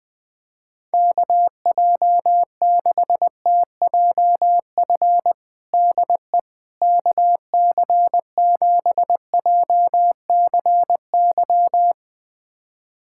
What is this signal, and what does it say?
Morse code, heard as a keyed tone, about 20 wpm, 700 Hz. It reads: KJ6TJF DE KC7JCY
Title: USA Callsign DE Callsign - 6 Characters - 20wpm